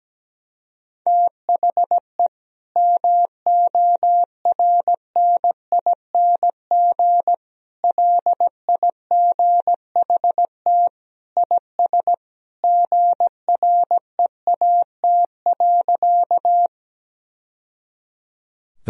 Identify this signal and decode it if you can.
Morse code, heard as a keyed tone, about 17 wpm, 700 Hz. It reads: THE MORNING LIGHT IS GREAT.
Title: Sentences from Top 1000 Words - Review Part 1 - 17wpm